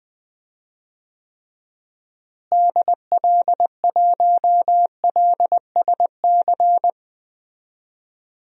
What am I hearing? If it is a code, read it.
Morse code, heard as a keyed tone, about 20 wpm, 700 Hz. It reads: DL1LSC
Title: DXCC-100 Call Signs - 6 Characters - Repeated 2x - 20wpm